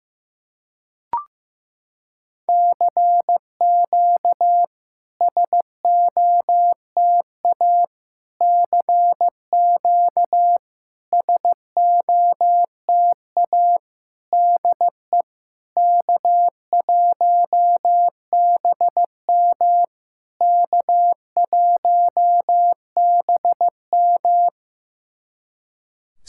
Morse code, heard as a keyed tone, about 15 wpm, 700 Hz. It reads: CQ SOTA CQ SOTA DE K1BM K1BM
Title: SOTA QSO Practice 15wpm